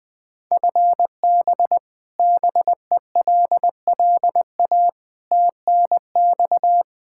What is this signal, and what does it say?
Morse code, heard as a keyed tone, about 20 wpm, 700 Hz. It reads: FB BELLA TNX
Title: Practice QSO - Exchange 3 - More Info - 20wpm